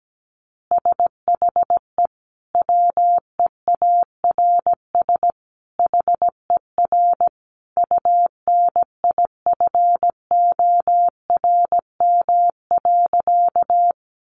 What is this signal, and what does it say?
Morse code, heard as a keyed tone, about 17 wpm, 700 Hz. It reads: SHE WEARS HER UNIFORM.
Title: Easy Phrases 17wpm